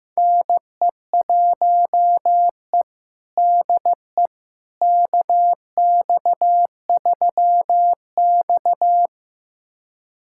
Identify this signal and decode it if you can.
Morse code, heard as a keyed tone, about 16 wpm, 700 Hz. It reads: NE1E DE KX3X
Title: USA Callsign DE Callsign - 4 Characters - 15wpm